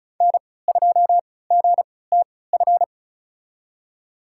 Morse code, heard as a keyed tone, about 35 wpm, spaced as for 20 wpm, 700 Hz. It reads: N2GTF
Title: USA Call Signs - 5 Characters - 3x Character Spacing - 35wpm